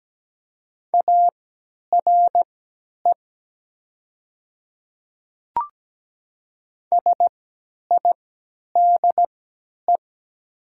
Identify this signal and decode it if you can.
Morse code, heard as a keyed tone, about 17 wpm, spaced as for 10 wpm, 700 Hz. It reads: ARE SIDE
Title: 3x Character Spacing - Top 100 Words - 17wpm